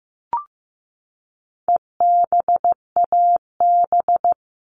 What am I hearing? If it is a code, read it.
Morse code, heard as a keyed tone, about 15 wpm, 700 Hz. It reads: EBAB
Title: All Cut Numbers - 4-Digit 15wpm